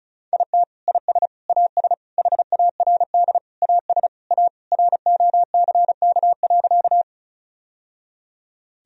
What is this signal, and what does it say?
Morse code, heard as a keyed tone, about 35 wpm, 700 Hz. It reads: IT IS AS HARD AS A ROCK.
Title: Sentences from Top 1000 Words - Review Part 2 - 35wpm